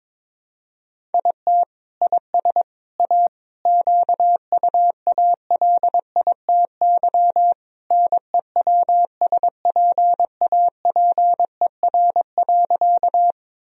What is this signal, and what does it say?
Morse code, heard as a keyed tone, about 22 wpm, 700 Hz. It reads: IT IS A QUALITY NEWSPAPER.
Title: Sentences from Top 1000 Words - Review Part 4 - 22wpm